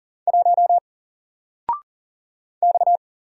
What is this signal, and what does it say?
Morse code, heard as a keyed tone, about 40 wpm, 700 Hz. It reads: 1 X
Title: Single Character (Letter:Number:Punctuation:Prosign) 40wpm